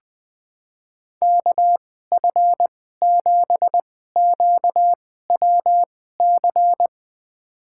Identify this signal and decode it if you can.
Morse code, heard as a keyed tone, about 21 wpm, spaced as for 15 wpm, 700 Hz. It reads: KF7QWC
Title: USA Call Signs - 6 Characters - 2x Character Spacing - 20wpm